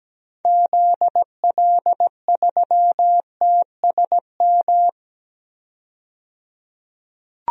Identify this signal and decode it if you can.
Morse code, heard as a keyed tone, about 17 wpm, 700 Hz. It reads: ZL3TSM
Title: DXCC-100 Call Signs - 6 Characters - 17wpm